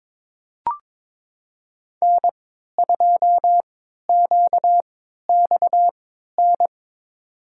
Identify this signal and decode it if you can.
Morse code, heard as a keyed tone, about 21 wpm, spaced as for 12 wpm, 700 Hz. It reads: N2QXN
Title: USA Call Signs - 5 Characters - 3x Character Spacing - 22wpm